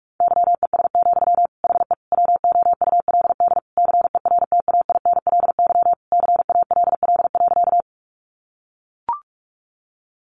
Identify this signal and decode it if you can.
Morse code, heard as a keyed tone, about 44 wpm, 700 Hz. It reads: YES, HE WOULD CERTAINLY CALL.
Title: Sentences from Top 900 Words 45wpm